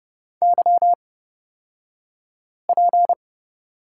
Morse code, heard as a keyed tone, about 30 wpm, spaced as for 5 wpm, 700 Hz. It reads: YP
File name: Lesson 37d - Letter x - Two Characters 30wpm